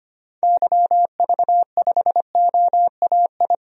Morse code, heard as a keyed tone, about 25 wpm, 700 Hz. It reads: YV5OAI
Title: DXCC-100 Call Signs - 6 Characters - Repeated 3x - 25wpm